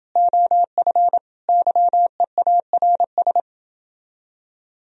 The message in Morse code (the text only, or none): OF YEARS